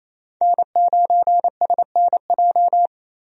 Morse code, heard as a keyed tone, about 28 wpm, 700 Hz. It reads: N9SNJ
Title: USA Call Signs - 5 Characters - 28wpm